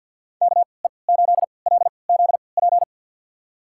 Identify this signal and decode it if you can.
Morse code, heard as a keyed tone, about 46 wpm, spaced as for 28 wpm, 700 Hz. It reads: KE8LZP
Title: USA Call Signs - 6 Characters - 3x Character Spacing - 50wpm